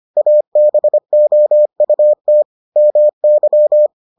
Morse code, heard as a keyed tone, about 25 wpm, 600 Hz. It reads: ABOUT MY